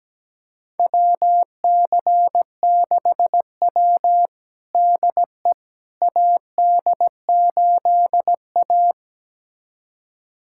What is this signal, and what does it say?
Morse code, heard as a keyed tone, about 17 wpm, 700 Hz. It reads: WC6W DE AD8A
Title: USA Callsign DE Callsign - 4 Characters - 17wpm